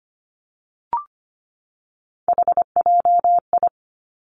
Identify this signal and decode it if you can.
Morse code, heard as a keyed tone, about 25 wpm, 700 Hz. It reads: HJI
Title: Three Challenging Letter-Number Combinations 25wpm